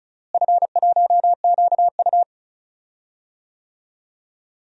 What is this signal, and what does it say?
Morse code, heard as a keyed tone, about 36 wpm, 700 Hz. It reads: F1QU
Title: DXCC-100 Call Signs - 4 Characters - Repeated 2x - 35wpm